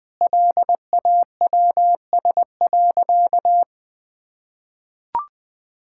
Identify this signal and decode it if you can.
Morse code, heard as a keyed tone, about 21 wpm, 700 Hz. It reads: LAWS.
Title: Sentences with 2-Word Phrases - Spoken Beginning - 20wpm